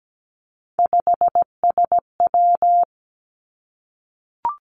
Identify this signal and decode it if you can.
Morse code, heard as a keyed tone, about 17 wpm, 700 Hz. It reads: 5SW
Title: Three Challenging Letter-Number Combinations 17wpm